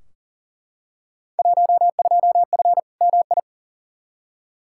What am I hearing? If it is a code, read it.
Morse code, heard as a keyed tone, about 40 wpm, 700 Hz. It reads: 12F MI
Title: Field Day Exchanges Encore - 40wpm